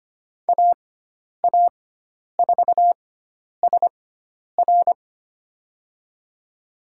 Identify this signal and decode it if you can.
Morse code, heard as a keyed tone, about 25 wpm, spaced as for 10 wpm, 700 Hz. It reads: AA4SR
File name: Lesson 16g - Letter c - Call Signs 25wpm